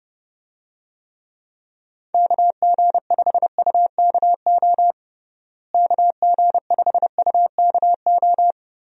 Morse code, heard as a keyed tone, about 32 wpm, 700 Hz. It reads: KG5UKO KG5UKO
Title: USA Call Signs - 6 Characters - Repeated 3x - 30wpm